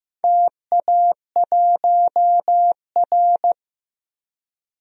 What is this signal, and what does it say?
Morse code, heard as a keyed tone, about 15 wpm, 700 Hz. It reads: TA1R
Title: DXCC-100 Call Signs - 4 Characters - Repeated 2x - 15wpm